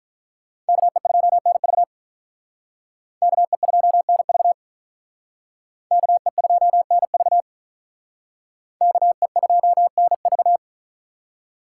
Morse code, heard as a keyed tone, about 45 wpm, 700 Hz. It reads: KE2NV KE2NV KE2NV KE2NV